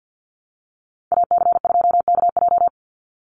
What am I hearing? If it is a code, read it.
Morse code, heard as a keyed tone, about 50 wpm, 700 Hz. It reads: AC2KJ
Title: Call Signs Repeated 2x - 50wpm